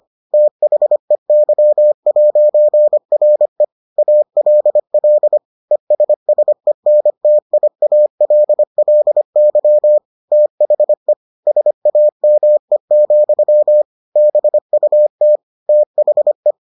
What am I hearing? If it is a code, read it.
Morse code, heard as a keyed tone, about 25 wpm, 600 Hz. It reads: THEY'RE ALL ESSENTIALLY THE SAME, BUT THE